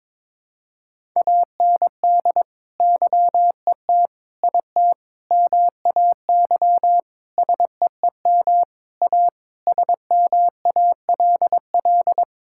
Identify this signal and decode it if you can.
Morse code, heard as a keyed tone, about 22 wpm, 700 Hz. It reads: AND YET IT MAY SEEM A SMALL
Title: Sentences from Top 1000 Words - Review Part 5 - 22wpm